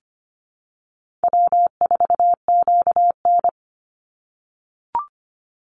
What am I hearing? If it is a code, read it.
Morse code, heard as a keyed tone, about 25 wpm, 700 Hz. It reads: W4QN
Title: Easy Call Signs - 25wpm